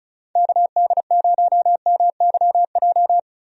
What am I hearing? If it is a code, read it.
Morse code, heard as a keyed tone, about 34 wpm, 700 Hz. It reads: KD0MYJ
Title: USA Call Signs - 6 Characters - Repeated 2x - 35wpm